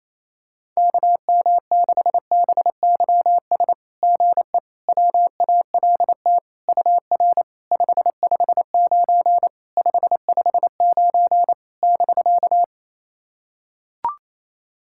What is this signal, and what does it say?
Morse code, heard as a keyed tone, about 29 wpm, 690 Hz. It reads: KM6BYS GE WALT UR 559 559 <BK>
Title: SOTA QSO Practice 28wpm